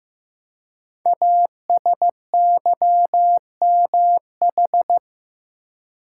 Morse code, heard as a keyed tone, about 15 wpm, 700 Hz. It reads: ASYMH
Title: Code Groups - 5 Letters - 15wpm